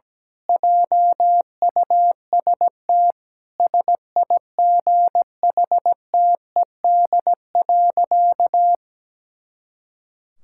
Morse code, heard as a keyed tone, about 17 wpm, 700 Hz. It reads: JUST SIGHTED.
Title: Pangram Challenge v2 - 17wpm